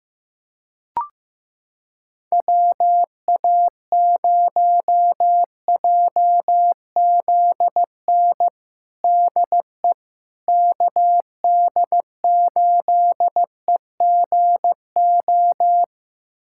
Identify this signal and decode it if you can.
Morse code, heard as a keyed tone, about 16 wpm, 700 Hz. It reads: WA0JZN DE KD8EGO
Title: USA Callsign DE Callsign 15wpm